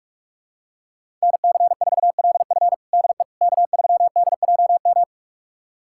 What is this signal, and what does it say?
Morse code, heard as a keyed tone, about 41 wpm, 700 Hz. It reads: NC4LF DE K3DJM
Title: USA Callsign DE Callsign - 5 Characters - 45wpm